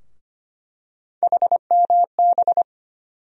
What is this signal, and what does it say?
Morse code, heard as a keyed tone, about 25 wpm, 700 Hz. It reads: HMB
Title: USA Call Sign Postfixes - 3 Letters 25wpm